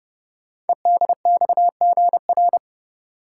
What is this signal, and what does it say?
Morse code, heard as a keyed tone, about 28 wpm, 700 Hz. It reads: EDXGR
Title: Code Groups - 5 Letters - 30wpm